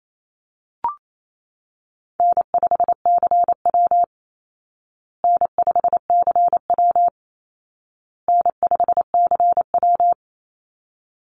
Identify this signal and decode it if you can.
Morse code, heard as a keyed tone, about 28 wpm, 700 Hz. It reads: N5CW N5CW N5CW
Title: Easy Call Signs Repeated 3x - 28wpm